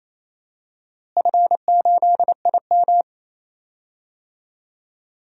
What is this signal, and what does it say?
Morse code, heard as a keyed tone, about 28 wpm, 700 Hz. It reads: F8IM
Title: DXCC-100 Call Signs - 4 Characters - Repeated 3x - 28wpm